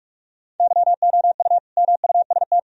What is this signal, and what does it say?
Morse code, heard as a keyed tone, about 41 wpm, 700 Hz. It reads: YOU MUST
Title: Sentences from Top 1000 Words - Review Part 2 - 45wpm